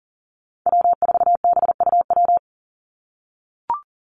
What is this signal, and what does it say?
Morse code, heard as a keyed tone, about 40 wpm, 700 Hz. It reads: W4BUW
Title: Call Signs - 40wpm